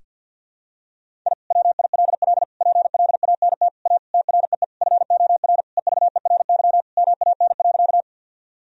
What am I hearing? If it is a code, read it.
Morse code, heard as a keyed tone, about 46 wpm, 700 Hz. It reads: I WILL PLANT A TREE FOR EVERY MAN.
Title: Sentences from Top 1000 Words - Review Part 1 - 50wpm